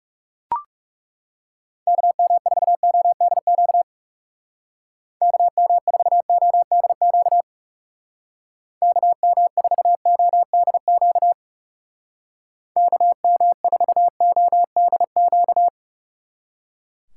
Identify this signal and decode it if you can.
Morse code, heard as a keyed tone, about 37 wpm, 700 Hz. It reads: KM4ODQ KM4ODQ KM4ODQ KM4ODQ
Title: Speed Racing - USA 2x3 Call Signs 30wpm